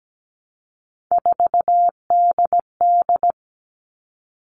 Morse code, heard as a keyed tone, about 17 wpm, 700 Hz. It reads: 4DD
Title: All Cut Numbers - 3-Digit 17wpm